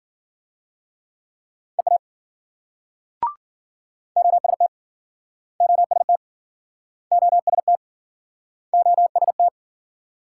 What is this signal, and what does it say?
Morse code, heard as a keyed tone, about 53 wpm, 700 Hz. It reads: EA OST OST OST OST